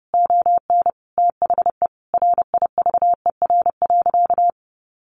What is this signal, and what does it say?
Morse code, heard as a keyed tone, about 30 wpm, 700 Hz. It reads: ON THE RIVER.
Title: Sentences from Top 1300 Words 30wpm